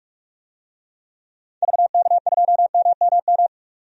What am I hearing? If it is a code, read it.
Morse code, heard as a keyed tone, about 47 wpm, 700 Hz. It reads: VK2MMM
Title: DXCC-100 Call Signs - 6 Characters - 45wpm